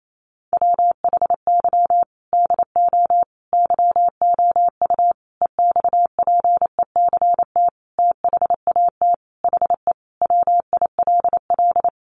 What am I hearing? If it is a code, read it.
Morse code, heard as a keyed tone, about 28 wpm, 700 Hz. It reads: WHY DO YOU EXPECT THAT HE WILL
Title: Sentences from Top 1100 Words 28wpm